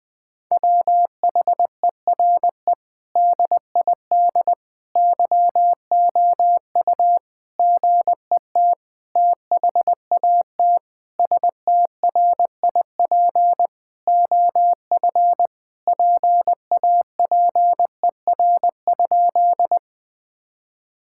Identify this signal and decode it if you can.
Morse code, heard as a keyed tone, about 20 wpm, 700 Hz. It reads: WHERE DID YOU GET THAT STRIP OF PAPER?
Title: Sentences from Top 1600 Words 20wpm